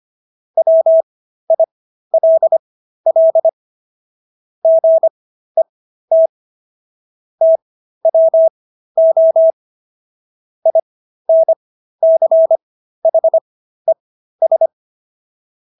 Morse code, heard as a keyed tone, about 25 wpm, spaced as for 13 wpm, 650 Hz. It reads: WILL GET TWO INCHES